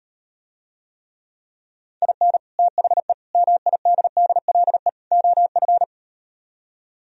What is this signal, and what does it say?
Morse code, heard as a keyed tone, about 38 wpm, 700 Hz. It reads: IN THE MIDDLE OF